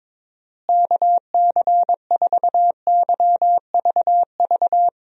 Morse code, heard as a keyed tone, about 22 wpm, 700 Hz. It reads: KC4YVV
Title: USA Call Signs - 6 Characters - Repeated 3x - 22wpm